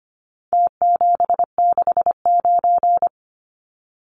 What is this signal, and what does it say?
Morse code, heard as a keyed tone, about 25 wpm, 700 Hz. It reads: T769
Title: Numbers - Contest Style - 4-Digit 25wpm